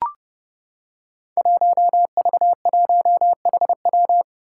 Morse code, heard as a keyed tone, about 29 wpm, 700 Hz. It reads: 1V1HW
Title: Five Challenging Letter-Number Combinations 30wpm